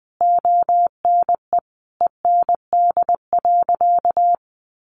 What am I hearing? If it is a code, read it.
Morse code, heard as a keyed tone, about 20 wpm, 700 Hz. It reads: ONE END.
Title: Easy Phrases 20wpm